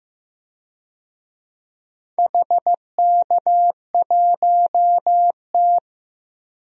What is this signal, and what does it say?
Morse code, heard as a keyed tone, about 15 wpm, 700 Hz. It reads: HK1T
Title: DXCC-100 Call Signs - 4 Characters - Repeated 3x - 15wpm